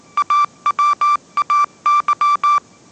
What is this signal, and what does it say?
Morse code, heard as a keyed tone, about 19 wpm, 1170 Hz. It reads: AWAY
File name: 20wpm